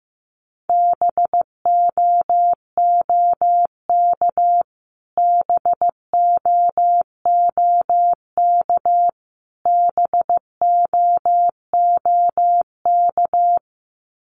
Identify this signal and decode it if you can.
Morse code, heard as a keyed tone, about 15 wpm, 700 Hz. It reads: BOOK BOOK BOOK
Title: Top 500 Words Repeated 15wpm